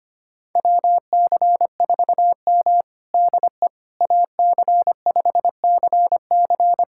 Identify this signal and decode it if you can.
Morse code, heard as a keyed tone, about 25 wpm, 700 Hz. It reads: WC4M DE AC5CC
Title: USA Callsign DE Callsign 25wpm